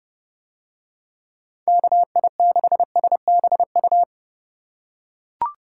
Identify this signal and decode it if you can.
Morse code, heard as a keyed tone, about 28 wpm, 700 Hz. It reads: KI6SBU